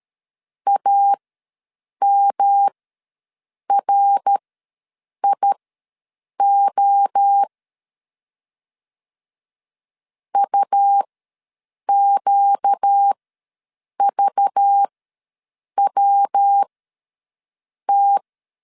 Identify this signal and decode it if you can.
Morse code, heard as a keyed tone, about 13 wpm, spaced as for 7 wpm, 800 Hz. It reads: AMRIO UQVWT
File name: LEZIONI / 5 LBVC / 5 LBV ws 12wpm 3015